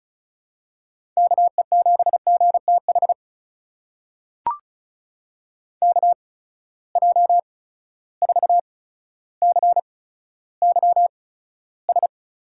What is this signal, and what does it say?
Morse code, heard as a keyed tone, about 34 wpm, spaced as for 15 wpm, 700 Hz. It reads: KE7GTH KJ4CYS
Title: USA Call Signs - 6 Characters - 8x Character Spacing - 35wpm